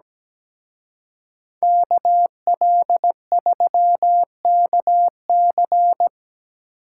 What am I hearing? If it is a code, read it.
Morse code, heard as a keyed tone, about 19 wpm, 700 Hz. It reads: KL3KC
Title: Speed Racing - USA 2x2 Call Signs 17wpm